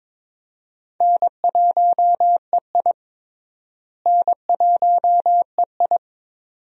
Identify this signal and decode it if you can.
Morse code, heard as a keyed tone, about 22 wpm, 700 Hz. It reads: N1EI N1EI
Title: USA Call Signs - 4 Characters - Repeated 2x - 22wpm